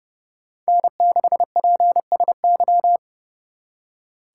Morse code, heard as a keyed tone, about 35 wpm, 700 Hz. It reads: N6PSY
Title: Speed Racing - USA 1x3 Call Signs 30wpm